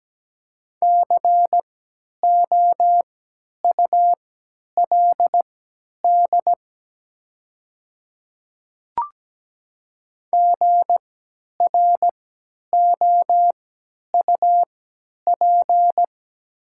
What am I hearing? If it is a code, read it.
Morse code, heard as a keyed tone, about 17 wpm, spaced as for 10 wpm, 700 Hz. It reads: COULD GROUP
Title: 3x Character Spacing - Top 300 Words - 17wpm